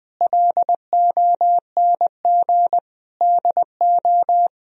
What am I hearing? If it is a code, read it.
Morse code, heard as a keyed tone, about 20 wpm, 700 Hz. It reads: LONG DO
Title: Sentences from Top 1000 Words - Review Part 4 - 20wpm